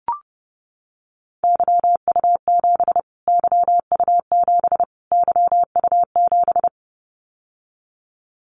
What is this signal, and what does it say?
Morse code, heard as a keyed tone, about 30 wpm, 700 Hz. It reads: YU7 YU7 YU7
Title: Callsign Prefixes for Top 25 DXCC - Repeated 3x - 30wpm